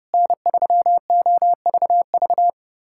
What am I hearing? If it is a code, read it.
Morse code, heard as a keyed tone, about 30 wpm, 700 Hz. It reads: N3OVV
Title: USA Callsign DE Callsign - 5 Characters - 30wpm